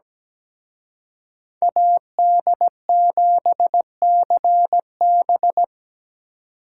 Morse code, heard as a keyed tone, about 20 wpm, 700 Hz. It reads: AD7CB
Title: Speed Racing - USA 2x2 Call Signs 17wpm